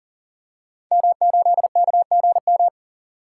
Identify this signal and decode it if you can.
Morse code, heard as a keyed tone, about 40 wpm, 700 Hz. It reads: M8KGM
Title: Code Groups - 5 Letter:Numbers - 40wpm